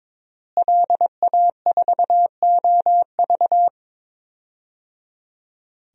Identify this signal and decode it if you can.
Morse code, heard as a keyed tone, about 22 wpm, 700 Hz. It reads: LA4OV
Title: DXCC-100 Call Signs - 5 Characters - Repeated 3x - 22wpm